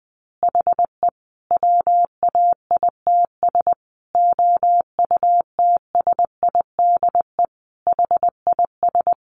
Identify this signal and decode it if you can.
Morse code, heard as a keyed tone, about 20 wpm, 700 Hz. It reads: HE WAITS OUTSIDE HIS
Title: Additional Easy Phrases 20wpm